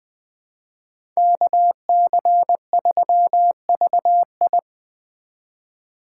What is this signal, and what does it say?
Morse code, heard as a keyed tone, about 21 wpm, 700 Hz. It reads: KC3VI
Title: Speed Racing - USA Call Signs 17wpm